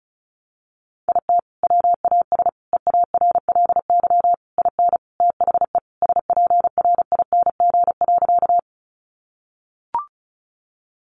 Morse code, heard as a keyed tone, about 34 wpm, 700 Hz. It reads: IT WAS EARLY IN THE SPRING.
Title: Sentences from Top 1200 Words 35wpm